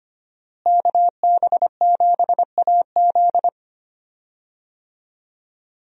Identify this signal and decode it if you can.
Morse code, heard as a keyed tone, about 25 wpm, 700 Hz. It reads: KB7AZ
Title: USA Call Signs - 5 Characters - Repeated 3x - 25wpm